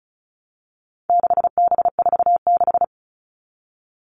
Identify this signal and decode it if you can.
Morse code, heard as a keyed tone, about 34 wpm, 700 Hz. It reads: 6B46
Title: All Cut Numbers - 4-Digit 35wpm